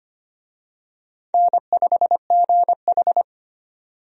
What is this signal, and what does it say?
Morse code, heard as a keyed tone, about 25 wpm, 700 Hz. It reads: N5GH
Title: USA Call Signs - 4 Characters - 25wpm